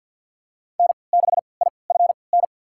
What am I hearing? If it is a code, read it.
Morse code, heard as a keyed tone, about 52 wpm, spaced as for 28 wpm, 700 Hz. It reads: N6IFN
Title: USA Call Signs - 5 Characters - 3x Character Spacing - 50wpm